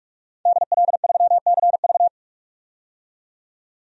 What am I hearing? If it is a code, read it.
Morse code, heard as a keyed tone, about 46 wpm, 700 Hz. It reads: DL3CV
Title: DXCC-100 Call Signs - 5 Characters - 45wpm